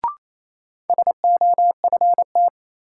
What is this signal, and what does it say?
Morse code, heard as a keyed tone, about 28 wpm, 700 Hz. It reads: SOFT
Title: Sentences from Top 1500 Words - New Word Drill Alt - 28wpm